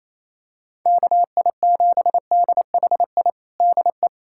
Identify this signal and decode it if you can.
Morse code, heard as a keyed tone, about 28 wpm, 700 Hz. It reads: KI7DHI DE
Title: USA Callsign DE Callsign - 6 Characters - 28wpm